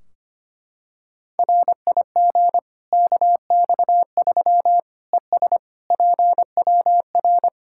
Morse code, heard as a keyed tone, about 25 wpm, 700 Hz. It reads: RIG KX3 ES PWR
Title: Practice QSO - Exchange 2 - Info - 25wpm